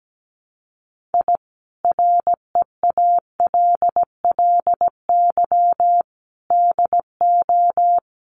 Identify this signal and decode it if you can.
Morse code, heard as a keyed tone, about 17 wpm, 700 Hz. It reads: I REALLY DO
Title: Sets of 5 Words Encore2 17wpm